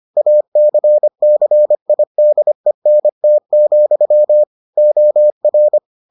Morse code, heard as a keyed tone, about 25 wpm, 600 Hz. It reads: ACCIDENT, OR